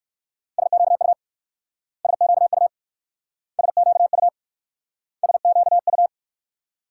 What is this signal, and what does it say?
Morse code, heard as a keyed tone, about 58 wpm, 700 Hz. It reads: SQU SQU SQU SQU